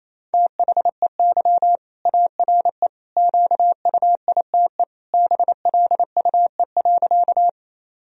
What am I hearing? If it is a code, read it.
Morse code, heard as a keyed tone, about 28 wpm, 700 Hz. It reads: THEY ARE QUITE BLUE.
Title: Sentences from Top 1000 Words - Review Part 3 - 28wpm